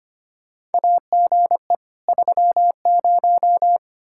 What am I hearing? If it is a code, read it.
Morse code, heard as a keyed tone, about 25 wpm, 700 Hz. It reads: AGE 30
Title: Practice QSO - Exchange 3 - More Info - 25wpm